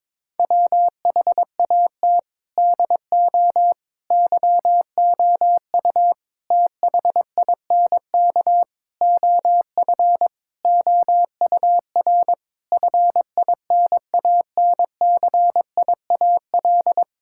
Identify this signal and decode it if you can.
Morse code, heard as a keyed tone, about 22 wpm, 700 Hz. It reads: WHAT DO YOU THINK OF OUR FINANCIAL
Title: Sentences from Top 1000 Words - Review Part 5 - 22wpm